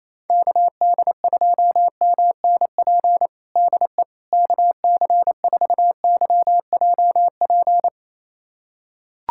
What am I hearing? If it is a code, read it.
Morse code, heard as a keyed tone, about 27 wpm, 700 Hz. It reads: KD2MNP DE KC4YJP
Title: USA Callsign DE Callsign 28wpm